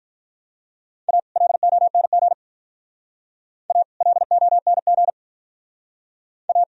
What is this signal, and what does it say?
Morse code, heard as a keyed tone, about 44 wpm, 700 Hz. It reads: A LONG A LONG A